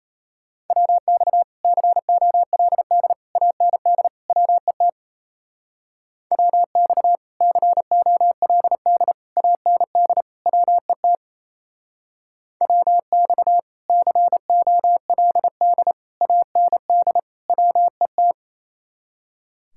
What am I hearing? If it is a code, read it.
Morse code, heard as a keyed tone, about 33 wpm, 700 Hz. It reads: WX COLD AND WET WX COLD AND WET WX COLD AND WET
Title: Speed-Racing Common QSO Phrases - 28wpm